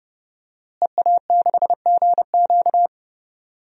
Morse code, heard as a keyed tone, about 30 wpm, 700 Hz. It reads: EA6GQ
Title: DXCC-100 Call Signs - 5 Characters - Repeated 2x - 30wpm